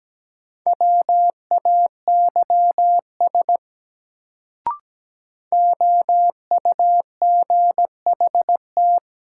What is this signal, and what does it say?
Morse code, heard as a keyed tone, about 17 wpm, 700 Hz. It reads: WAYS OUGHT
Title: Sets of 5 Words Encore - New Word Drill - 17wpm